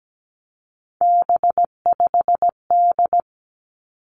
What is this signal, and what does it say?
Morse code, heard as a keyed tone, about 17 wpm, 700 Hz. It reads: B5D
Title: Three Challenging Letter-Number Combinations 17wpm